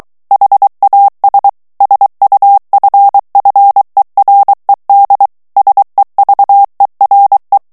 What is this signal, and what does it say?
Morse code, heard as a keyed tone, about 23 wpm, 800 Hz. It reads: HAS SUFFERED SEVERE
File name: morse mp3 / ANGLAIS texte clair 2 vitesse 22 mots minute